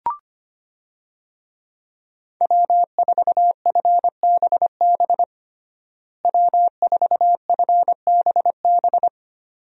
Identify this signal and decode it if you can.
Morse code, heard as a keyed tone, about 25 wpm, 700 Hz. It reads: W4FBB W4FBB
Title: USA Call Signs - 5 Characters - Repeated 2x - 25wpm